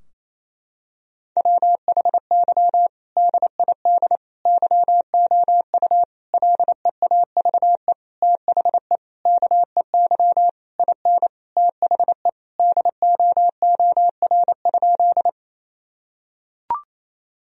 Morse code, heard as a keyed tone, about 27 wpm, 700 Hz. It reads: WHY DID YOU LEAVE THE KEY IN THE DOOR?
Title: Sentences from Top 1000 Words - Review Part 4 - 28wpm